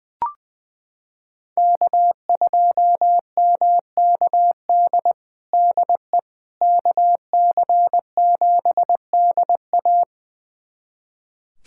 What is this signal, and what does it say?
Morse code, heard as a keyed tone, about 20 wpm, 700 Hz. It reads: K2MKD DE KC7DA
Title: USA Callsign DE Callsign - 5 Characters - 20wpm